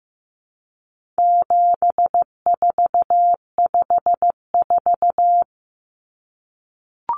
Morse code, heard as a keyed tone, about 15 wpm, 700 Hz. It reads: 7454
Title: Numbers - 4-Digit 15wpm